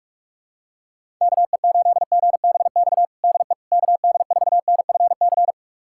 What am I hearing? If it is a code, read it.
Morse code, heard as a keyed tone, about 43 wpm, 700 Hz. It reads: KE8GBX DE KD4NFC
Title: USA Callsign DE Callsign - 6 Characters - 45wpm